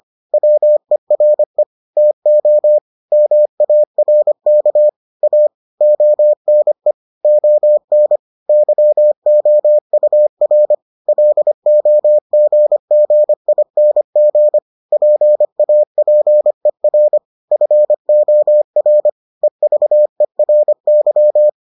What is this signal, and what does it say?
Morse code, heard as a keyed tone, about 25 wpm, 600 Hz. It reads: WERE TO MARK A ONE ON YOUR LOGGING PAPER FOR EVERY